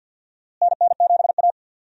Morse code, heard as a keyed tone, about 52 wpm, 700 Hz. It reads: NN7A
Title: USA Call Signs - 4 Characters - Repeated 2x - 50wpm